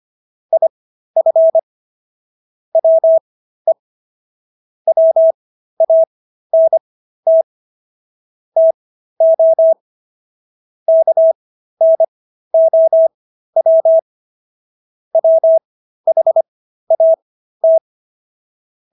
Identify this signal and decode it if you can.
Morse code, heard as a keyed tone, about 25 wpm, spaced as for 13 wpm, 650 Hz. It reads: IF WE WANT TO KNOW WHAT